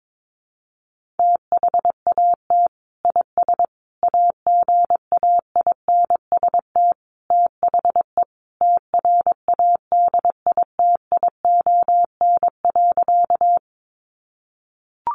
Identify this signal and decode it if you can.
Morse code, heard as a keyed tone, about 22 wpm, 700 Hz. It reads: THAT IS AGAINST THE TRADITION.
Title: Sentences from Top 1500 Words 22wpm